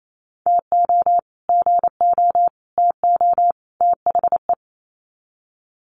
Morse code, heard as a keyed tone, about 28 wpm, 700 Hz. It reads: TO GO TO THE
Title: Sets of 4 Words 28wpm